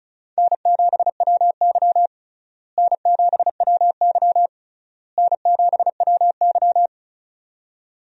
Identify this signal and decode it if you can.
Morse code, heard as a keyed tone, about 36 wpm, 700 Hz. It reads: N7WY N7WY N7WY
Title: USA Call Signs - 4 Characters - Repeated 3x - 35wpm